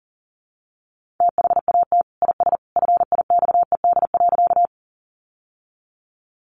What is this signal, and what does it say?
Morse code, heard as a keyed tone, about 40 wpm, 700 Hz. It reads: THAT IS FIXED.
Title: Sentences from Top 1100 Words 40wpm